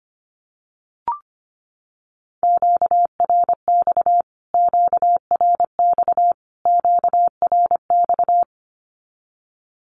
Morse code, heard as a keyed tone, about 25 wpm, 700 Hz. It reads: QRX QRX QRX
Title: Q-Codes Repeated 3x 25wpm